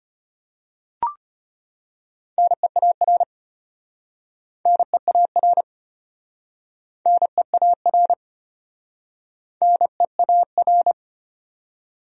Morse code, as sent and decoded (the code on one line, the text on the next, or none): -. .- .-. / -. .- .-. / -. .- .-. / -. .- .-.
NEAR NEAR NEAR NEAR